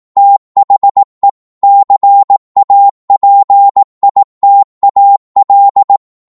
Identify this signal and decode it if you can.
Morse code, heard as a keyed tone, about 18 wpm, 800 Hz. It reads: THE CAPITAL